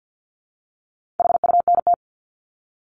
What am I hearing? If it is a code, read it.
Morse code, heard as a keyed tone, about 50 wpm, 700 Hz. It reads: HUNT
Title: Top 1501-1750 Words 50wpm